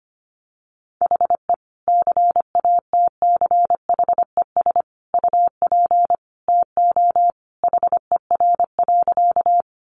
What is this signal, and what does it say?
Morse code, heard as a keyed tone, about 25 wpm, 700 Hz. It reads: HE CATCHES UP TO HER.
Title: Additional Easy Phrases 25wpm